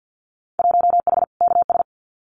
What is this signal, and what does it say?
Morse code, heard as a keyed tone, about 50 wpm, 700 Hz. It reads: 1H KS
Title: Winter Field Day Exchanges 50wpm